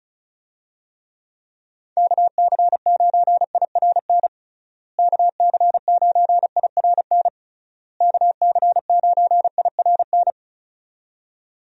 Morse code, heard as a keyed tone, about 34 wpm, 700 Hz. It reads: KC9IRN KC9IRN KC9IRN
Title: USA Call Signs - 6 Characters - Repeated 3x - 35wpm